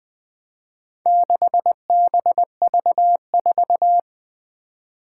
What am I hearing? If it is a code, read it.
Morse code, heard as a keyed tone, about 20 wpm, 700 Hz. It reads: 6BV4
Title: Four Challenging Letter-Number Combinations 20wpm